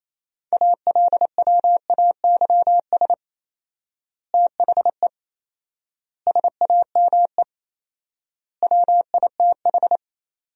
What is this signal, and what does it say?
Morse code, heard as a keyed tone, about 28 wpm, 700 Hz. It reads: ALWAYS THE SAME WITH
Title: 4x Word-Spacing - Sentences from Top 300 Words - 28wpm